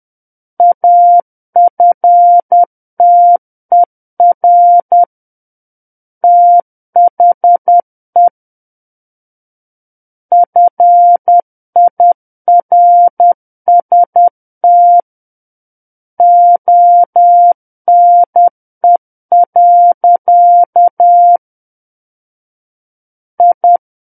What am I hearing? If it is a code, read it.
Morse code, heard as a keyed tone, about 10 wpm, 700 Hz. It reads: AFTER THE FIRST ONE. I
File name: Book One / 01-Chapter 1 - 10 WPM